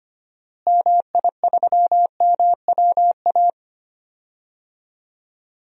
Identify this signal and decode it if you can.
Morse code, heard as a keyed tone, about 25 wpm, 700 Hz. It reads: MI3MWA
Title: DXCC-100 Call Signs - 6 Characters - Repeated 3x - 25wpm